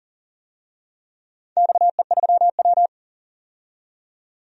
Morse code, heard as a keyed tone, about 40 wpm, 700 Hz. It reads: XE3W
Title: DXCC-100 Call Signs - 4 Characters - 40wpm